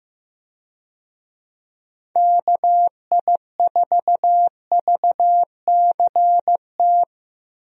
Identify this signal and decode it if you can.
Morse code, heard as a keyed tone, about 15 wpm, 700 Hz. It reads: KI4VCT
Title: USA Call Signs - 6 Characters - Repeated 3x - 15wpm